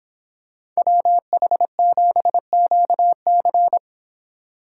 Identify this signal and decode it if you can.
Morse code, heard as a keyed tone, about 25 wpm, 700 Hz. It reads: WH7QC